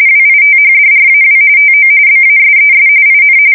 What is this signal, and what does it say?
RTTY, about 45.45 baud. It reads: WITHDRAW TO BASES IN
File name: fsk170 / USS-New-Jersey